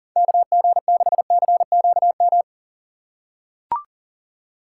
Morse code, heard as a keyed tone, about 40 wpm, 700 Hz. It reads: KG6CQM
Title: USA Call Signs - 6 Characters - 4x Character Spacing - 40wpm